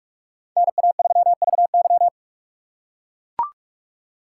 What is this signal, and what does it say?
Morse code, heard as a keyed tone, about 42 wpm, 700 Hz. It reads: NA3VY